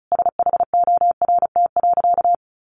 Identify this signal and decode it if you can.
Morse code, heard as a keyed tone, about 34 wpm, 700 Hz. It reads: SHORT.
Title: Sentences from Top 1400 Words 35wpm